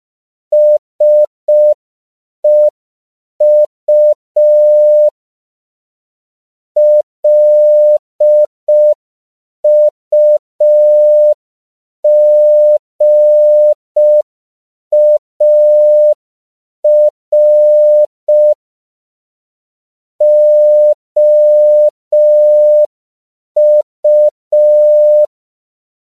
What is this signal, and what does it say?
Morse code, heard as a keyed tone, about 5 wpm, 600 Hz. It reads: SEU LUGAR OU